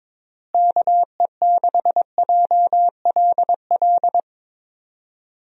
Morse code, heard as a keyed tone, about 22 wpm, 700 Hz. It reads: KE6JLL
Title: USA Call Signs - 6 Characters - Repeated 3x - 22wpm